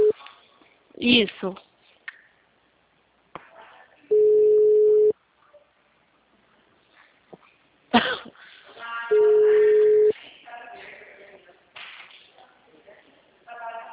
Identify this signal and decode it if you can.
Ringback tone (Europe, 425 Hz), with speech.